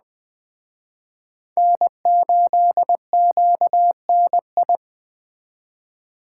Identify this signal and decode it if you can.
Morse code, heard as a keyed tone, about 20 wpm, 700 Hz. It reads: N8QNI
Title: Speed Racing - USA 1x3 Call Signs 17wpm